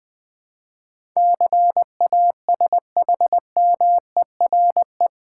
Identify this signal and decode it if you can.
Morse code, heard as a keyed tone, about 20 wpm, 700 Hz. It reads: CASHMERE
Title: Difficult Words with Hints - 20wpm